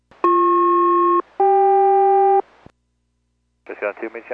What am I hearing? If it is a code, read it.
Aircraft SELCAL, heard as two pairs of tones, with speech. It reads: BP-CK